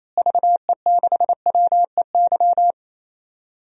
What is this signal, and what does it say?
Morse code, heard as a keyed tone, about 27 wpm, 700 Hz. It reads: VE6WEY
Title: DXCC-100 Call Signs - 6 Characters - Repeated 2x - 28wpm